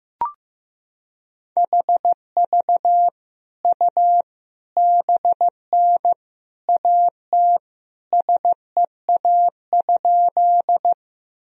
Morse code, heard as a keyed tone, about 15 wpm, 700 Hz. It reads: HV U BN AT SEA?
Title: Sentences from Top 300 Words and Abbreviations 15wpm